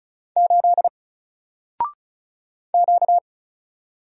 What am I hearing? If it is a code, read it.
Morse code, heard as a keyed tone, about 36 wpm, 700 Hz. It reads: 8 Q
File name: Single-Letter-Number-35wpm